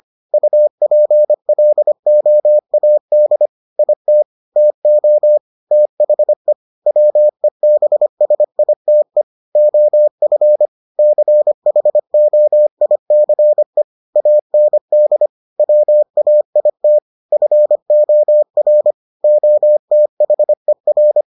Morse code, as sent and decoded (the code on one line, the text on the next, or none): ..- .--. .-.. --- .- -.. / .. - / - --- / - .... / .-- . -... ... .. - . / --- ..-. / -.-. .... --- .. -.-. / .- -. -.. / .-- .- .. - / ..-. --- .-. / --- - .... .-.
UPLOAD IT TO THE WEBSITE OF CHOICE AND WAIT FOR OTHER